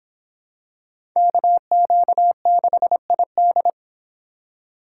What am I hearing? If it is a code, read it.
Morse code, heard as a keyed tone, about 25 wpm, 700 Hz. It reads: KQ6ID